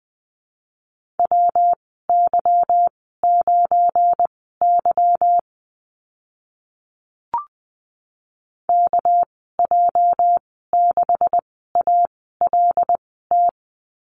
Morse code, heard as a keyed tone, about 20 wpm, spaced as for 15 wpm, 700 Hz. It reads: WY9Y KJ6ALT
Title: 231 Lesson 37g - Letter x - Call Signs 20wpm